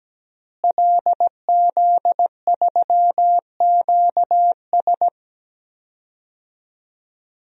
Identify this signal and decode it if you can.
Morse code, heard as a keyed tone, about 17 wpm, 700 Hz. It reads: LZ3QS
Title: DXCC-100 Call Signs - 5 Characters - Repeated 3x - 17wpm